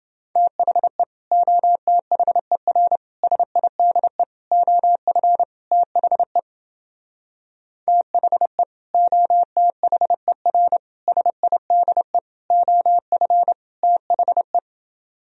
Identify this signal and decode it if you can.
Morse code, heard as a keyed tone, about 28 wpm, 700 Hz. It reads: THE OTHER SIDE OF THE THE OTHER SIDE OF THE